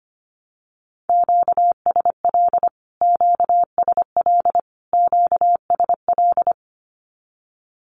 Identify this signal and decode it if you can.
Morse code, heard as a keyed tone, about 25 wpm, 700 Hz. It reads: QSL QSL QSL
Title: Q-Codes Repeated 3x 25wpm